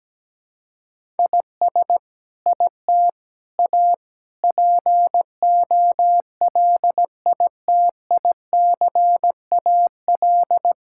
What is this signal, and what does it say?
Morse code, heard as a keyed tone, about 17 wpm, 700 Hz. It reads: IS IT A POLITICAL
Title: Sentences from Top 1000 Words - Review Part 5 - 17wpm